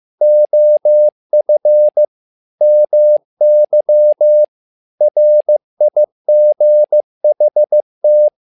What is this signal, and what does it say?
Morse code, heard as a keyed tone, about 15 wpm, 600 Hz. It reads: OF MY RIGHT
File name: learning center / tower trouble / tower trouble 15wpm